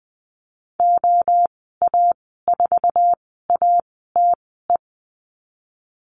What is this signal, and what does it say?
Morse code, heard as a keyed tone, about 19 wpm, spaced as for 14 wpm, 700 Hz. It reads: OA4ATE
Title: 161 Lesson 26g - Letter g - Call Signs 20wpm